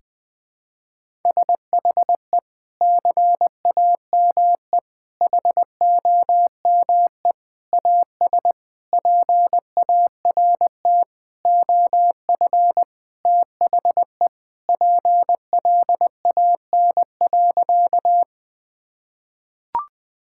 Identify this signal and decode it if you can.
Morse code, heard as a keyed tone, about 20 wpm, 700 Hz. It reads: SHE CAME HOME AS PART OF THE PLAN.
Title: Sentences with 4-Word Phrases - Spoken Beginning - 20wpm